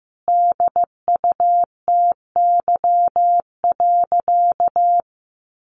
Morse code, heard as a keyed tone, about 15 wpm, 700 Hz. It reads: DUTY.
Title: Sentences from Top 1500 Words 15wpm